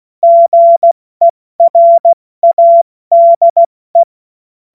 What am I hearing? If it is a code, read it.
Morse code, heard as a keyed tone, about 16 wpm, 680 Hz. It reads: GERADE
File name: Morsekurs / Witze / Witz 20. September 2022 70 F80